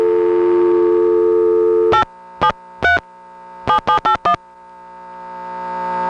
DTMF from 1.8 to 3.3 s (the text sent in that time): 873